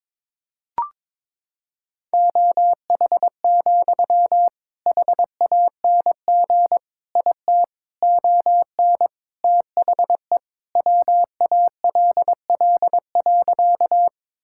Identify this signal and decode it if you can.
Morse code, heard as a keyed tone, about 22 wpm, 700 Hz. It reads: OH, HANG IT ON THE WALL.
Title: Sentences from Top 1600 Words 22wpm